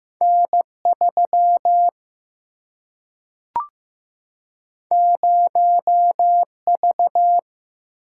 Morse code, heard as a keyed tone, about 15 wpm, 700 Hz. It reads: N3 0V
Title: Single-Letter Single-Number 15wpm